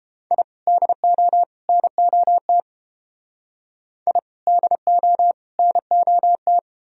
Morse code, heard as a keyed tone, about 32 wpm, 700 Hz. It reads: I DO NOT I DO NOT